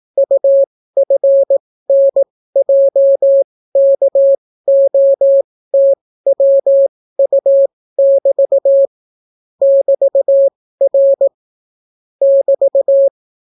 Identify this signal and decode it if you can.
Morse code, heard as a keyed tone, about 18 wpm, spaced as for 14 wpm, 550 Hz. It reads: UFNJKOTWU<BT> <BT>R <BT>